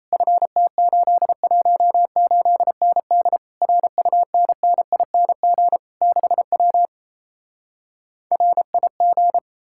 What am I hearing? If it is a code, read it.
Morse code, heard as a keyed tone, about 37 wpm, 700 Hz. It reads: FT818ND RUNNING 6W RIG